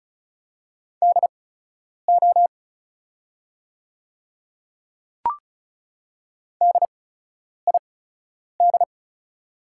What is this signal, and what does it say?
Morse code, heard as a keyed tone, about 34 wpm, spaced as for 10 wpm, 700 Hz. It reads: DO DID
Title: 8x Character Spacing - Top 300 Words - 35wpm